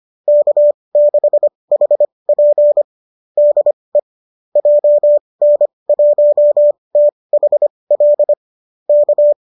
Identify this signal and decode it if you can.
Morse code, heard as a keyed tone, about 25 wpm, spaced as for 20 wpm, 590 Hz. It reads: K6HP DE JN1THL K